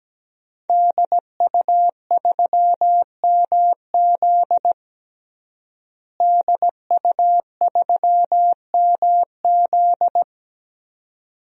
Morse code, heard as a keyed tone, about 17 wpm, 700 Hz. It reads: DU3MZ DU3MZ
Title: DXCC-100 Call Signs - 5 Characters - Repeated 2x - 17wpm